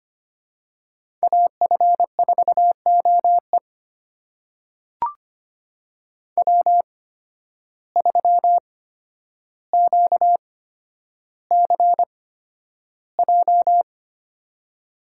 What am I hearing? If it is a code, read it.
Morse code, heard as a keyed tone, about 25 wpm, spaced as for 11 wpm, 700 Hz. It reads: AF4OE W3QCJ
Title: USA Call Signs - 5 Characters - 8x Character Spacing - 25wpm